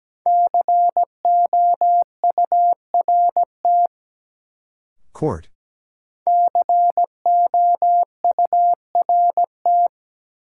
Morse code, heard as a keyed tone, about 17 wpm, 700 Hz. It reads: COURT COURT
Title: Sentences from Top 1600 Words - New Word Drill - 17wpm